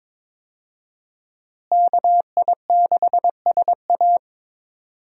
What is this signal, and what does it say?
Morse code, heard as a keyed tone, about 21 wpm, 700 Hz. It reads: KI6SA
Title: USA Call Signs - 5 Characters - Repeated 2x - 22wpm